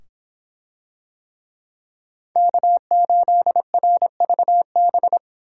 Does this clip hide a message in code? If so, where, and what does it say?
Yes: Morse code from 2.4 to 5.2 s, heard as a keyed tone, about 24 wpm, 700 Hz. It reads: K8RVB